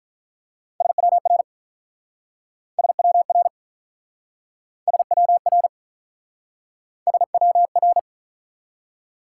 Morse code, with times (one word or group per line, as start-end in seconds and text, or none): 0.8-1.4 SWR
2.8-3.5 SWR
4.9-5.7 SWR
7.1-8.0 SWR